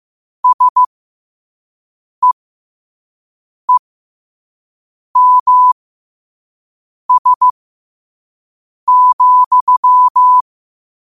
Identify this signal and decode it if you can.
Morse code, heard as a keyed tone, about 15 wpm, spaced as for 5 wpm, 1000 Hz. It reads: SEEMS,
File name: qotdfiles / 5wpm